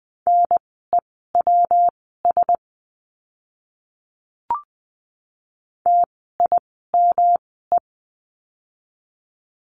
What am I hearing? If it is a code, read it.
Morse code, heard as a keyed tone, about 20 wpm, spaced as for 15 wpm, 700 Hz. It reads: NEWS TIME
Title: 115 Lesson 19f - Letter w - Words 20wpm